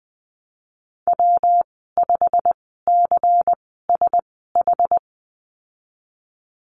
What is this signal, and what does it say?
Morse code, heard as a keyed tone, about 20 wpm, spaced as for 15 wpm, 700 Hz. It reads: W5CSH
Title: 193 Lesson 31g - Letter v - Call Signs 20wpm